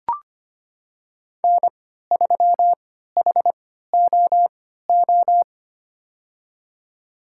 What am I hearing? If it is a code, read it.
Morse code, heard as a keyed tone, about 25 wpm, spaced as for 14 wpm, 700 Hz. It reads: N3HOO
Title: USA Call Signs - 5 Characters - 3x Character Spacing - 25wpm